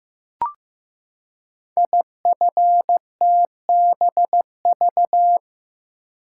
Morse code, heard as a keyed tone, about 15 wpm, 700 Hz. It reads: IFTBV
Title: Code Groups - 5 Letters - 15wpm